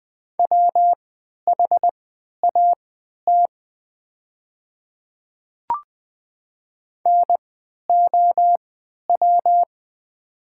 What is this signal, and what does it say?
Morse code, heard as a keyed tone, about 20 wpm, spaced as for 11 wpm, 700 Hz. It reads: WHAT NOW
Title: 3x Character Spacing - Top 100 Words - 20wpm